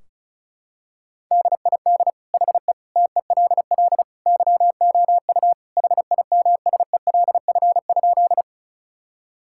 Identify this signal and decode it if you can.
Morse code, heard as a keyed tone, about 34 wpm, 700 Hz. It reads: DID HE TELL YOU HIMSELF?
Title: Sentences from Top 1600 Words 35wpm